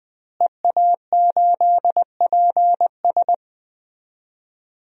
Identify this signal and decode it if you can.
Morse code, heard as a keyed tone, about 20 wpm, 700 Hz. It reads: EA8PS
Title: DXCC-100 Call Signs - 5 Characters - Repeated 3x - 20wpm